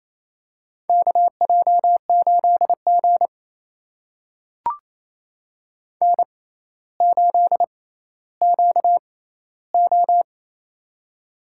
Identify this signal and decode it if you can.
Morse code, heard as a keyed tone, about 28 wpm, spaced as for 14 wpm, 700 Hz. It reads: KJ8G N8QO
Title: USA Call Signs - 4 Characters - 6x Character Spacing - 28wpm